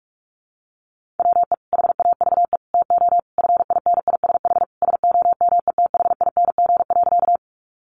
Morse code, heard as a keyed tone, about 44 wpm, 700 Hz. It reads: WE HAVE TO FINISH SOMETHING.
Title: Sentences from Top 1200 Words 45wpm